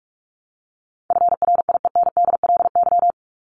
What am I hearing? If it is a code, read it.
Morse code, heard as a keyed tone, about 44 wpm, 700 Hz. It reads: FRIENDLY
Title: Top 1751-2000 Words 45wpm